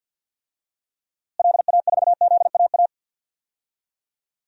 Morse code, heard as a keyed tone, about 50 wpm, 700 Hz. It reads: LA4ZAA
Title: DXCC-100 Call Signs - 6 Characters - Repeated 2x - 50wpm